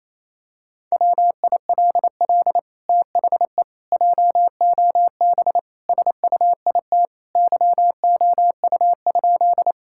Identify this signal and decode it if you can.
Morse code, heard as a keyed tone, about 28 wpm, 700 Hz. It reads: WILL THE JOB SUIT YOU?
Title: Sentences from Top 1600 Words 28wpm